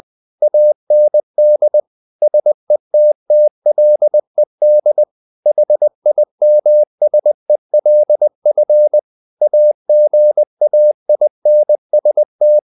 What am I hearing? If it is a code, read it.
Morse code, heard as a keyed tone, about 20 wpm, 600 Hz. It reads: AND SETTLED HIMSELF AGAINST
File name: the old man and the sea / the old man and the sea part 1 20wpm